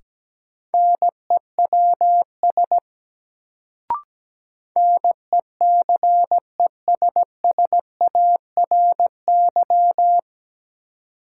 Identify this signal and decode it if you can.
Morse code, heard as a keyed tone, about 17 wpm, 700 Hz. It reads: NEWS NECESSARY
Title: Sentences from Top 800 Words - New Word Drill - 17wpm